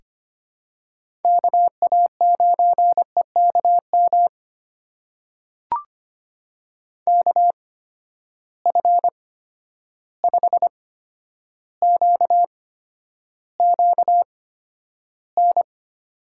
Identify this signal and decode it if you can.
Morse code, heard as a keyed tone, about 25 wpm, spaced as for 11 wpm, 700 Hz. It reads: KA9EKM KF5QQN
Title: USA Call Signs - 6 Characters - 8x Character Spacing - 25wpm